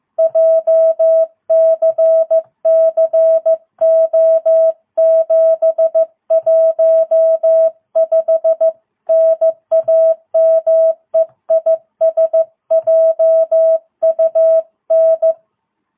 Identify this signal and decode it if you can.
Morse code, heard as a keyed tone, about 14 wpm, 660 Hz. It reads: JCCO715 NAME IS JUN
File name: jh8cbh / 2022cw